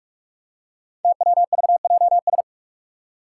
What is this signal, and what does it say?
Morse code, heard as a keyed tone, about 44 wpm, 700 Hz. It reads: TWVJS
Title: Five Challenging Letter-Number Combinations 45wpm